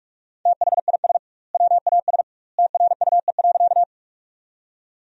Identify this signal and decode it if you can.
Morse code, heard as a keyed tone, about 41 wpm, 700 Hz. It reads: THIS WAS TRUE.
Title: 2x Word-Spacing - Sentences from Top 400 Words - 45wpm